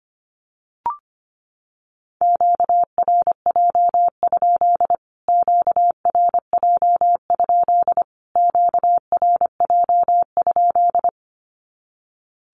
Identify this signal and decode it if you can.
Morse code, heard as a keyed tone, about 25 wpm, 700 Hz. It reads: QRJ? QRJ? QRJ?
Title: Q-Codes Repeated 3x 25wpm